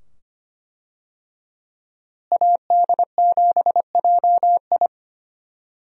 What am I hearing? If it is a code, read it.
Morse code, heard as a keyed tone, about 25 wpm, 690 Hz. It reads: AD7JI
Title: USA Call Signs - 5 Characters - Repeated 2x - 25wpm